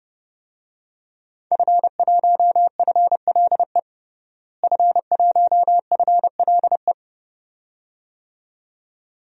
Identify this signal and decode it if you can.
Morse code, heard as a keyed tone, about 30 wpm, 700 Hz. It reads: F1FLE F1FLE
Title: DXCC-100 Call Signs - 5 Characters - Repeated 2x - 30wpm